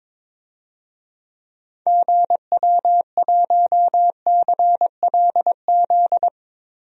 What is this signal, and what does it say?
Morse code, heard as a keyed tone, about 22 wpm, 700 Hz. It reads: GW1CLZ
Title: DXCC-100 Call Signs - 6 Characters - Repeated 3x - 22wpm